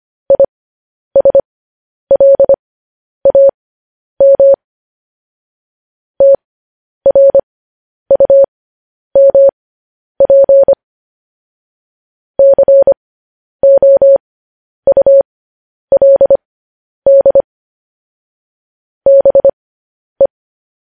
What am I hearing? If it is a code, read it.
Morse code, heard as a keyed tone, about 25 wpm, spaced as for 10 wpm, 550 Hz. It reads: ISLAM TRUMP COULD BE
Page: news headlines in morse code at 10 wpm